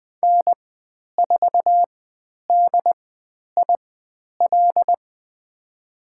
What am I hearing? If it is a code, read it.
Morse code, heard as a keyed tone, about 20 wpm, spaced as for 10 wpm, 700 Hz. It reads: N4DIL
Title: 102 Lesson 17g - Letter u - Call Signs 20wpm